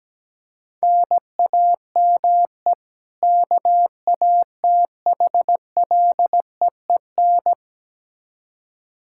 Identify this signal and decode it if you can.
Morse code, heard as a keyed tone, about 17 wpm, 700 Hz. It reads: NAME KATHLEEN
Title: Practice QSO - Exchange 1 - Beginning - 17wpm